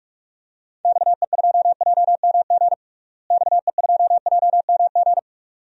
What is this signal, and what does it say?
Morse code, heard as a keyed tone, about 44 wpm, 700 Hz. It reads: XE2JMG XE2JMG
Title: DXCC-100 Call Signs - 6 Characters - Repeated 2x - 45wpm